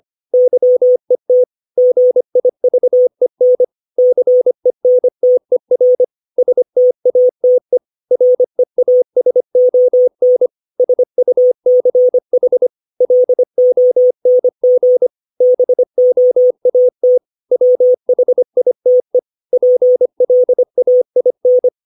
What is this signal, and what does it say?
Morse code, heard as a keyed tone, about 25 wpm, 500 Hz. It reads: YET GIVEN CENTER STATE REASON SUCH LONG BOAT WHITE PLAIN
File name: 632 words / 632 Words 25WPM